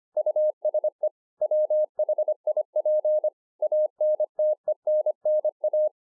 Morse code, heard as a keyed tone, about 25 wpm, 610 Hz. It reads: USE WHIP ANTENNA
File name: 2011-04-mp3 / BEGINNERS DIARY 25WPM